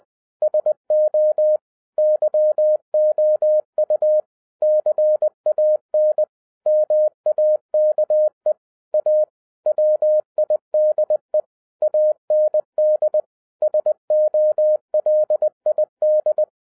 Morse code, heard as a keyed tone, about 20 wpm, 610 Hz. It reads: SO YOU CAN MAKE A WIDE AND SOLID
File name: 2011-10-mp3 / n13-TIPS FROM MY 34 YEARS OF EXPERIENCE 20wpm